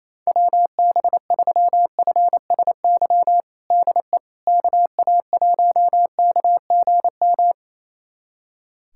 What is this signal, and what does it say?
Morse code, heard as a keyed tone, about 28 wpm, 700 Hz. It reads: WB3FSY DE KA1KGM
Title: USA Callsign DE Callsign - 6 Characters - 28wpm